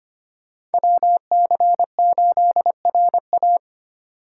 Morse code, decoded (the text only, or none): WC8RA